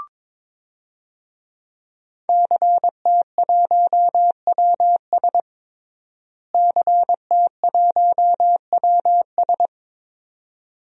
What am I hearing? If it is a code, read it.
Morse code, heard as a keyed tone, about 22 wpm, 700 Hz. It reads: CT1WS CT1WS
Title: DXCC-100 Call Signs - 5 Characters - Repeated 2x - 22wpm